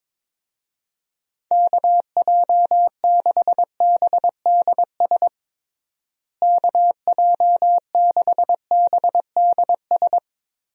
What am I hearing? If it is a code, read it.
Morse code, heard as a keyed tone, about 22 wpm, 700 Hz. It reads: KJ6BDS KJ6BDS
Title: USA Call Signs - 6 Characters - Repeated 2x - 22wpm